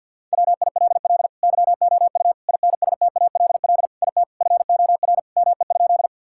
Morse code, heard as a keyed tone, about 50 wpm, 700 Hz. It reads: WILL YOU INSTALL IT FOR ME?
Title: Sentences from Top 1600 Words 50wpm